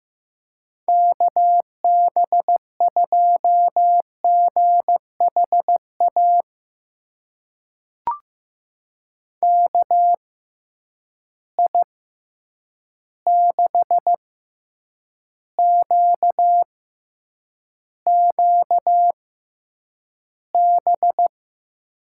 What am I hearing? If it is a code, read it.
Morse code, heard as a keyed tone, about 15 wpm, spaced as for 8 wpm, 700 Hz. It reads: KB2GHA KI6QQB
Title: USA Call Signs - 6 Characters - 6x Character Spacing - 15wpm